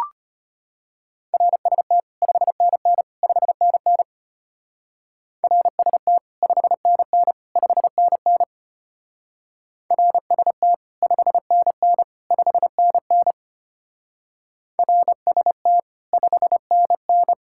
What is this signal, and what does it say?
Morse code, heard as a keyed tone, about 32 wpm, 700 Hz. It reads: RST 5NN 5NN RST 5NN 5NN RST 5NN 5NN RST 5NN